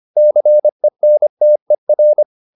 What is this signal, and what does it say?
Morse code, heard as a keyed tone, about 25 wpm, 600 Hz. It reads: CENTER